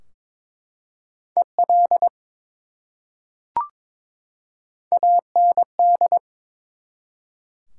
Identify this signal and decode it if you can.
Morse code, heard as a keyed tone, about 22 wpm, 700 Hz. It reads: EL AND
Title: Top 180 QSO Elements 22wpm